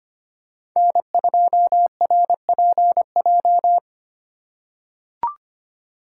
Morse code, heard as a keyed tone, about 25 wpm, 700 Hz. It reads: N2RPJ
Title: USA Call Signs - 5 Characters - 4x Character Spacing - 25wpm